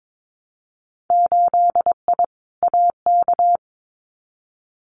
Morse code, heard as a keyed tone, about 22 wpm, 700 Hz. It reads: 8I AK
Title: Winter Field Day Exchanges Encore 22wpm